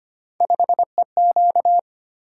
Morse code, heard as a keyed tone, about 25 wpm, 700 Hz. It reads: 5EQ
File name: Three Letter-Number Combinations - 25wpm